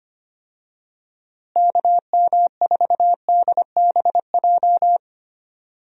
Morse code, heard as a keyed tone, about 25 wpm, 700 Hz. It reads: KM4DBJ
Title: USA Call Signs - 6 Characters - Repeated 2x - 25wpm